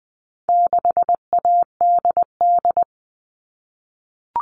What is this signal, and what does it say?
Morse code, heard as a keyed tone, about 20 wpm, 700 Hz. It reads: 6ADD
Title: All Cut Numbers - 4-Digit 20wpm